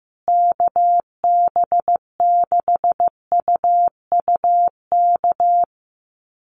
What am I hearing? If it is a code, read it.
Morse code, heard as a keyed tone, about 15 wpm, 700 Hz. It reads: KB6UUK
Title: USA Call Signs Repeated 3x 15wpm